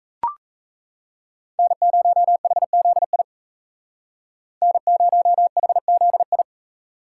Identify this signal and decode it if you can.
Morse code, heard as a keyed tone, about 35 wpm, 700 Hz. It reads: N0HZI N0HZI
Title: Speed Racing - USA 1x3 Call Signs 28wpm